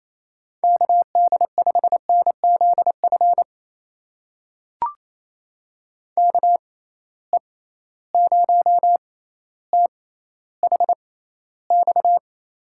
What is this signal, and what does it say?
Morse code, heard as a keyed tone, about 28 wpm, spaced as for 14 wpm, 700 Hz. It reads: KD5NZF KE0THX
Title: USA Call Signs - 6 Characters - 6x Character Spacing - 28wpm